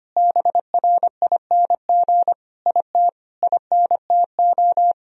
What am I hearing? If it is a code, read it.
Morse code, heard as a keyed tone, about 25 wpm, 700 Hz. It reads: BRING IT INTO
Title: Sentences from Top 1000 Words - Review Part 4 - 25wpm